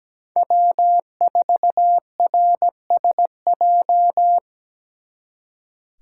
Morse code, heard as a keyed tone, about 20 wpm, 700 Hz. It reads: W4RSJ
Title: Speed Racing - USA 1x3 Call Signs 17wpm